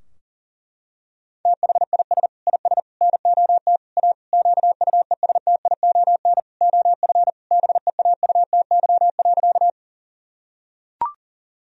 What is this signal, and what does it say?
Morse code, heard as a keyed tone, about 40 wpm, 700 Hz. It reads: THIS IS NOT A QUESTION OF BEAUTY.
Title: Sentences from Top 1000 Words - Review Part 2 - 40wpm